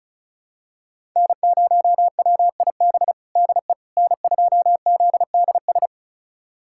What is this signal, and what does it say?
Morse code, heard as a keyed tone, about 34 wpm, 700 Hz. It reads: N0WIB DE N2ZDS
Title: USA Callsign DE Callsign - 5 Characters - 35wpm